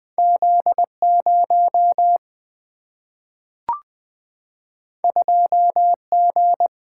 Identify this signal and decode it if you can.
Morse code, heard as a keyed tone, about 20 wpm, 700 Hz. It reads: Z 0 2 G
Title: Single-Letter Single-Number 20wpm